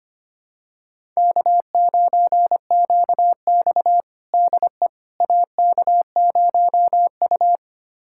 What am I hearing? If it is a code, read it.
Morse code, heard as a keyed tone, about 25 wpm, 700 Hz. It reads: K9QX DE AK0U
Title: USA Callsign DE Callsign - 4 Characters - 25wpm